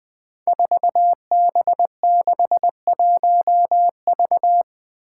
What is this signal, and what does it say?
Morse code, heard as a keyed tone, about 20 wpm, 700 Hz. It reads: 4B61V
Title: Five Challenging Letter-Number Combinations 20wpm